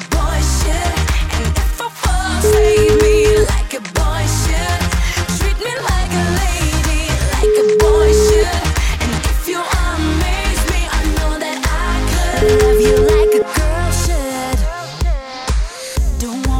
Ringback tone (Europe, 425 Hz), with noise.